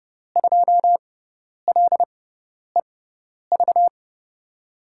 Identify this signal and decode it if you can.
Morse code, heard as a keyed tone, about 30 wpm, spaced as for 10 wpm, 700 Hz. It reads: 2LEV
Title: USA Call Signs - 6 Characters - 6x Character Spacing - 30wpm